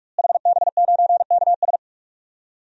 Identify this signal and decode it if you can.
Morse code, heard as a keyed tone, about 46 wpm, 700 Hz. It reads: HB9KS
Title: DXCC-100 Call Signs - 5 Characters - Repeated 2x - 45wpm